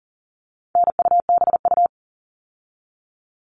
Morse code, heard as a keyed tone, about 40 wpm, 700 Hz. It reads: NUBU
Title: All Cut Numbers - 4-Digit 40wpm